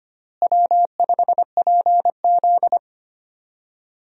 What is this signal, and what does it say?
Morse code, heard as a keyed tone, about 25 wpm, 700 Hz. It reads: W5PZ
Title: USA Call Signs - 4 Characters - Repeated 3x - 25wpm